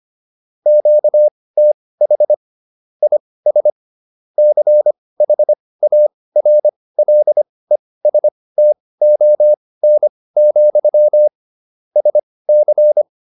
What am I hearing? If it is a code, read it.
Morse code, heard as a keyed tone, about 25 wpm, spaced as for 18 wpm, 600 Hz. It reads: QTH IS CHARLESTON, SC